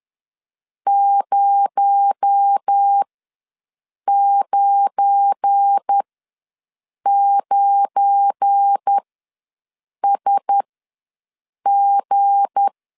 Morse code, heard as a keyed tone, about 10 wpm, spaced as for 6 wpm, 800 Hz. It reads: 099SG
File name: LEZIONI / 7 GF9Z / 7 GF9 ws 10wpm 4034